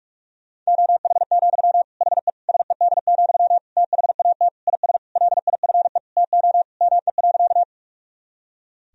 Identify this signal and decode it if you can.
Morse code, heard as a keyed tone, about 41 wpm, 700 Hz. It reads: OH, HE SED, THAT IS LIFE TO ME.
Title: Sentences from Top 500 Words and Abbreviations 45wpm